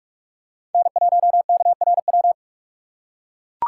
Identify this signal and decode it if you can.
Morse code, heard as a keyed tone, about 43 wpm, 700 Hz. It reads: N1KRW
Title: USA Call Signs - 5 Characters - 2x Character Spacing - 45wpm